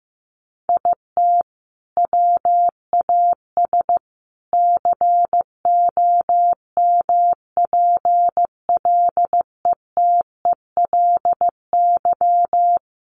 Morse code, heard as a keyed tone, about 15 wpm, 700 Hz. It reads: IT WAS COMPLETELY
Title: Sentences from Top 1100 Words 15wpm